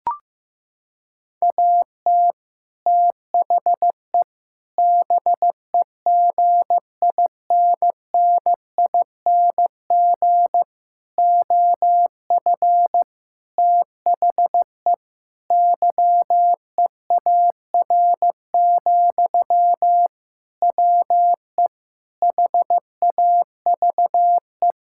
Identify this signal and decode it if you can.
Morse code, heard as a keyed tone, about 15 wpm, 700 Hz. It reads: AT THE BEGINNING OF THE YEAR, WE HAVE
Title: Sentences from Top 1000 Words - Review Part 5 - 15wpm